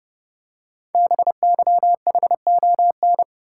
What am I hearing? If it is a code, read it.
Morse code, heard as a keyed tone, about 30 wpm, 700 Hz. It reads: BYHON
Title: Code Groups - 5 Letters - 30wpm